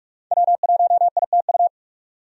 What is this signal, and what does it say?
Morse code, heard as a keyed tone, about 46 wpm, 700 Hz. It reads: W1ITU
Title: USA Call Signs - 5 Characters - 45wpm